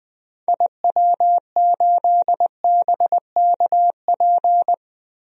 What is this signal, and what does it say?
Morse code, heard as a keyed tone, about 20 wpm, 700 Hz. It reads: IW8BKP
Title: DXCC-100 Call Signs - 6 Characters - 20wpm